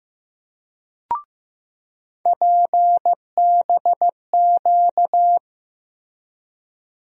Morse code, heard as a keyed tone, about 15 wpm, 700 Hz. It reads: PBQ
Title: USA Call Sign Postfixes - 3 Letters 15wpm